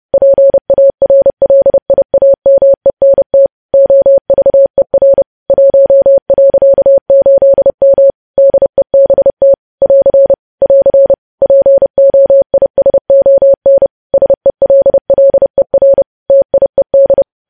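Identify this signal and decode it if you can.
Morse code, heard as a keyed tone, about 30 wpm, 550 Hz. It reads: PARLIAMENT OVER 1.8M DEBT <AR> <AR> POISON SELLER TIED
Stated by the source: Morse news headlines at 30 wpm.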